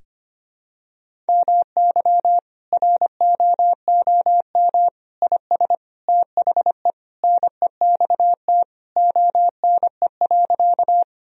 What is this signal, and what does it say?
Morse code, heard as a keyed tone, about 25 wpm, 700 Hz. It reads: MY ROOM IS THE NEXT ONE.
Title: Sentences from Top 1000 Words - Review Part 1 - 25wpm